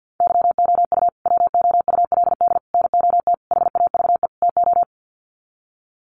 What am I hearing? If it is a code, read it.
Morse code, heard as a keyed tone, about 48 wpm, 700 Hz. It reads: YOU WOULD NOT HAVE TO
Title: Sets of 5 Words Encore 50wpm